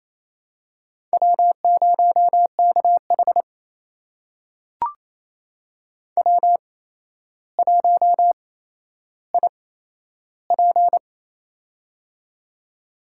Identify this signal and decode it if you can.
Morse code, heard as a keyed tone, about 27 wpm, spaced as for 12 wpm, 700 Hz. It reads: W0KH W1IP
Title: USA Call Signs - 4 Characters - 8x Character Spacing - 28wpm